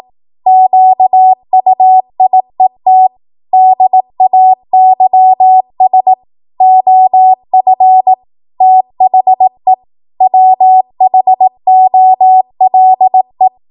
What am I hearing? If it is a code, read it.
Morse code, heard as a keyed tone, about 18 wpm, 750 Hz. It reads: QUIET DAYS OF THE WHOLE
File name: Morse / 18WPM